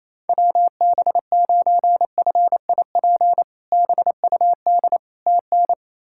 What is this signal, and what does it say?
Morse code, heard as a keyed tone, about 27 wpm, 700 Hz. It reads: WB9FIP BUD TN
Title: K1USN SST QSO Practice - 28wpm